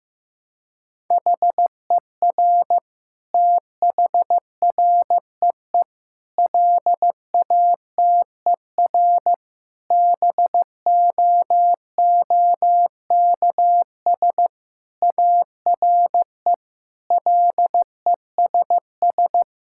Morse code, heard as a keyed tone, about 15 wpm, 700 Hz. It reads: HER THREE LATER BOOKS ARE LESS
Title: Sentences from Top 1000 Words - Review Part 6 - 15wpm